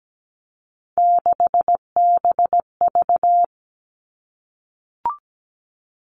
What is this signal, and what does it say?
Morse code, heard as a keyed tone, about 17 wpm, 700 Hz. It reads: 6BV
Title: Three Challenging Letter-Number Combinations 17wpm